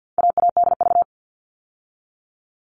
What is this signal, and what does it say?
Morse code, heard as a keyed tone, about 50 wpm, 700 Hz. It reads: AADV
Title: All Cut Numbers - 4-Digit 50wpm